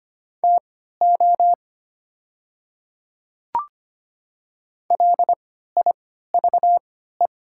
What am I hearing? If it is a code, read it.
Morse code, heard as a keyed tone, about 25 wpm, spaced as for 14 wpm, 700 Hz. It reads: TO LIVE
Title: 3x Character Spacing - Top 200 Words - 25wpm